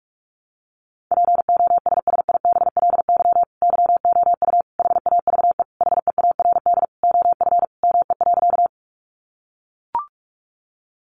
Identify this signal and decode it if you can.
Morse code, heard as a keyed tone, about 44 wpm, 700 Hz. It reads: POSSIBLY YOU HAVE HEARD OF ME.
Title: Sentences from Top 1300 Words 45wpm